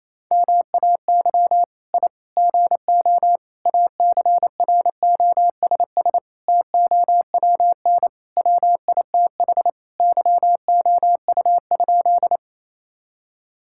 Morse code, heard as a keyed tone, about 27 wpm, 700 Hz. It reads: MAY I GO ACROSS TOWN WITH YOU?
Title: Sentences from Top 1000 Words - Review Part 5 - 28wpm